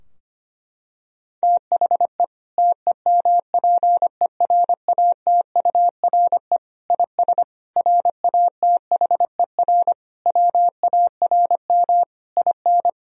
Morse code, heard as a keyed tone, about 25 wpm, 700 Hz. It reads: THE TEMPERATURE IS RATHER WARM IN
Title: Sentences from Top 1000 Words - Review Part 4 - 25wpm